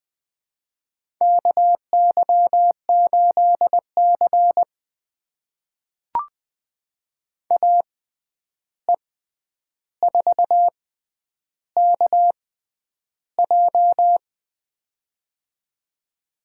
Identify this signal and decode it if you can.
Morse code, heard as a keyed tone, about 20 wpm, spaced as for 10 wpm, 700 Hz. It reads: KY8C AE4KJ
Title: USA Call Signs - 6x Character Spacing - 20wpm